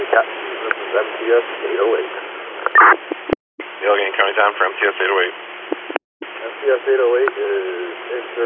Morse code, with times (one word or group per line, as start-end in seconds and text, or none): none